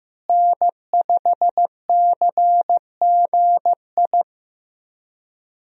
Morse code, heard as a keyed tone, about 15 wpm, 700 Hz. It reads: N5CGI
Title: USA Call Signs - 5 Characters - Repeated 2x - 15wpm